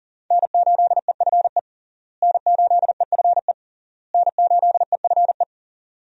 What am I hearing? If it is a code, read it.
Morse code, heard as a keyed tone, about 40 wpm, 700 Hz. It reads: N8EFE N8EFE N8EFE
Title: USA Call Signs - 5 Characters - Repeated 3x - 40wpm